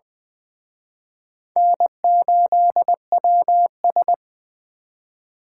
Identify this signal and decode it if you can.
Morse code, heard as a keyed tone, about 20 wpm, 700 Hz. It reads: N8WS
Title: Speed Racing - USA 1x2 Call Signs 17wpm